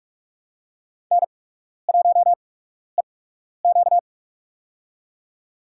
Morse code, heard as a keyed tone, about 48 wpm, spaced as for 12 wpm, 700 Hz. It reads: N1EQ
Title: USA Call Signs - 4 Characters - 8x Character Spacing - 45wpm